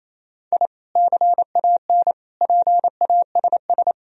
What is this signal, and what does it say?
Morse code, heard as a keyed tone, about 28 wpm, 700 Hz. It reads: I CAN PASS
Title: Sentences from Top 1000 Words - Review Part 1 - 28wpm